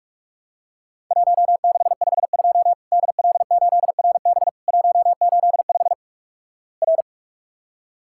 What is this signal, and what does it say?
Morse code, heard as a keyed tone, about 46 wpm, 700 Hz. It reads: 1652 DL8RB 185 R
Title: Worked All Europe DX Contest - QTC Exchange - 45wpm